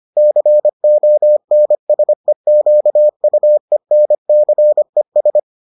Morse code, heard as a keyed tone, about 25 wpm, 600 Hz. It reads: CONSEQUENCES